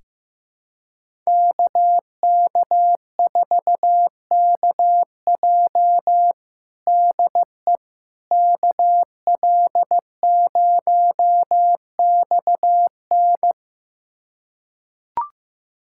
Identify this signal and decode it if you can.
Morse code, heard as a keyed tone, about 15 wpm, 700 Hz. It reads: KK4KJ DE KL0XN
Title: USA Callsign DE Callsign - 5 Characters - 15wpm